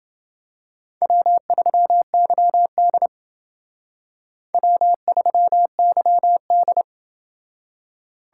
Morse code, heard as a keyed tone, about 29 wpm, 700 Hz. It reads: W3YD W3YD